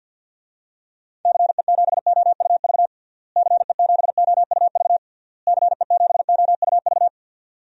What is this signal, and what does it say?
Morse code, heard as a keyed tone, about 51 wpm, 700 Hz. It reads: CE7OUV CE7OUV CE7OUV
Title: DXCC-100 Call Signs - 6 Characters - Repeated 3x - 50wpm